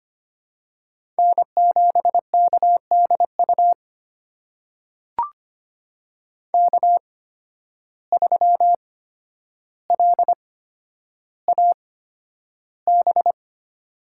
Morse code, heard as a keyed tone, about 25 wpm, spaced as for 11 wpm, 700 Hz. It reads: N7KDU K3LAB
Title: USA Call Signs - 5 Characters - 8x Character Spacing - 25wpm